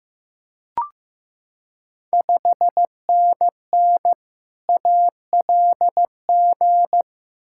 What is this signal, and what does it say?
Morse code, heard as a keyed tone, about 15 wpm, 700 Hz. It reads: 5NN ALG
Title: Ontario QSO Party Exchange 15wpm